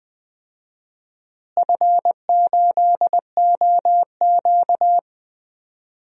Morse code, heard as a keyed tone, about 20 wpm, 700 Hz. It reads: F8OQ
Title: DXCC-100 Call Signs - 4 Characters - Repeated 3x - 20wpm